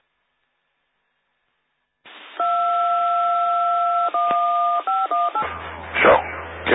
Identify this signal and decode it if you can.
DTMF, heard as telephone touch tones, tones of about 210 ms, with speech. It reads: 21518